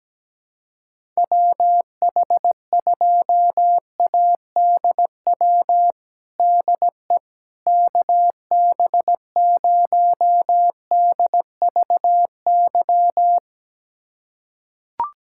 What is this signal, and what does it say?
Morse code, heard as a keyed tone, about 17 wpm, 700 Hz. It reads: WH2ADW DE KB0DVY
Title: USA Callsign DE Callsign - 6 Characters - 17wpm